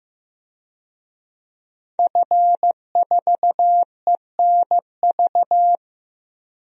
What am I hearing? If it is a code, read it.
Morse code, heard as a keyed tone, about 16 wpm, 700 Hz. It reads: F4ENV
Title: DXCC-100 Call Signs - 5 Characters - Repeated 2x - 15wpm